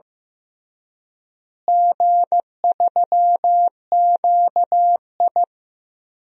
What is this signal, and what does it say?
Morse code, heard as a keyed tone, about 15 wpm, 700 Hz. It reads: G3QI
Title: DXCC-100 Call Signs - 4 Characters - Repeated 3x - 15wpm